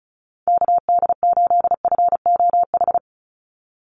Morse code, heard as a keyed tone, about 34 wpm, 700 Hz. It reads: KD8FOH
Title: USA Call Signs 35wpm